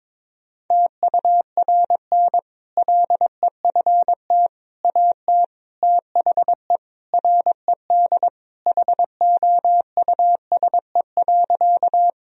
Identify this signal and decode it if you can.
Morse code, heard as a keyed tone, about 22 wpm, 700 Hz. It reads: TURN LEFT AT THE RED HOUSE.
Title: Sentences from Top 1000 Words - Review Part 2 - 22wpm